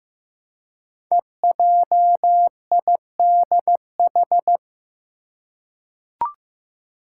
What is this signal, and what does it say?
Morse code, heard as a keyed tone, about 15 wpm, 700 Hz. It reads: EJIDH
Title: Code Groups - 5 Letters - 15wpm